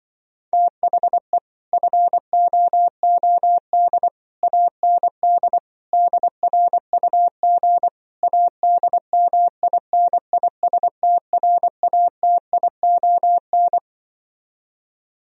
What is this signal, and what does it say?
Morse code, heard as a keyed tone, about 24 wpm, 700 Hz. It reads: THE FOOD AND DRUG ADMINISTRATION
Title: Speed-Racing - Sets of 5 Words 20wpm